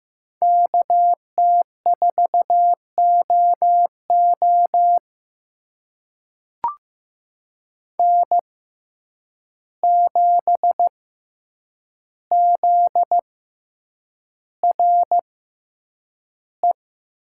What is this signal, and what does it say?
Morse code, heard as a keyed tone, about 15 wpm, spaced as for 8 wpm, 700 Hz. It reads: KT4OO N7ZRE
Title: USA Call Signs - 5 Characters - 6x Character Spacing - 15wpm